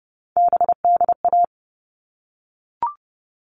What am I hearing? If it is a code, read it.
Morse code, heard as a keyed tone, about 30 wpm, 700 Hz. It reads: BDA
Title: All Cut Numbers - 3-Digit 30wpm